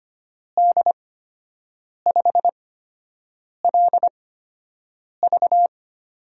Morse code, heard as a keyed tone, about 25 wpm, spaced as for 7 wpm, 700 Hz. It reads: D5LV
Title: USA Call Signs - 5 Characters - 8x Character Spacing - 25wpm